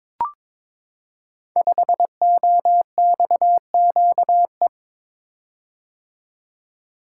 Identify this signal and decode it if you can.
Morse code, heard as a keyed tone, about 22 wpm, 690 Hz. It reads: 5OXQE
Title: Code Groups - 5 Letter:Numbers - 22wpm